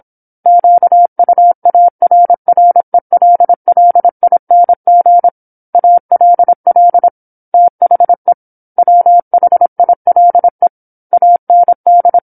Morse code, heard as a keyed tone, about 27 wpm, 700 Hz. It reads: QUARRELLING ALL THE WHILE AND
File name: 08-Chapter-08 26 WPM